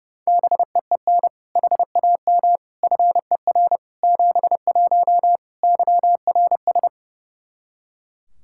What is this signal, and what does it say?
Morse code, heard as a keyed tone, about 30 wpm, 700 Hz. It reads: BEEN HAM FER 71 YRS
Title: Practice QSO - Exchange 3 - More Info - 30wpm